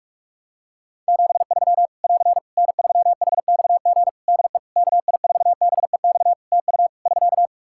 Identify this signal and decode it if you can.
Morse code, heard as a keyed tone, about 46 wpm, 700 Hz. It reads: 73 <AR> N3HXG DE KI4BEX TU <SK>
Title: K1USN SST QSO Practice 45wpm